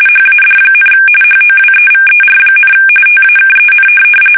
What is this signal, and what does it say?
RTTY, about 45.45 baud. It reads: HAD, HOWEVER, TAKEN
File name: fsk850 / benjamin-g-lamme